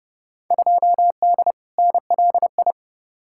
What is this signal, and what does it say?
Morse code, heard as a keyed tone, about 31 wpm, 700 Hz. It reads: 2D NLI
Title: Field Day Exchanges - 30wpm